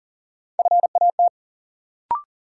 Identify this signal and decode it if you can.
Morse code, heard as a keyed tone, about 40 wpm, 700 Hz. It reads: FAT
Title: Sentences from Top 600 Words - New Word Drill - 40wpm